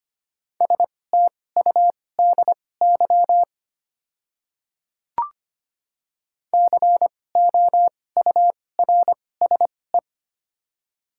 Morse code, heard as a keyed tone, about 25 wpm, spaced as for 18 wpm, 700 Hz. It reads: STUDY COURSE
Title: 2x Character Spacing - Top 500 Words - 25wpm